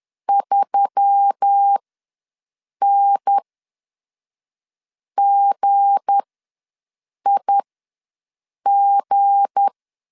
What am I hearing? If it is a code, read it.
Morse code, heard as a keyed tone, about 10 wpm, spaced as for 5 wpm, 800 Hz. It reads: 3NGIG
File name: LEZIONI / 7 GF9Z / 7 G ws 10wpm 3843